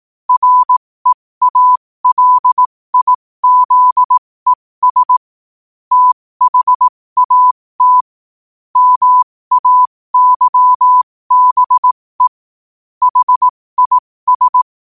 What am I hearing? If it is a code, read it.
Morse code, heard as a keyed tone, about 18 wpm, spaced as for 15 wpm, 1000 Hz. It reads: REALIZES THAT MAYBE HIS
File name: 15wpm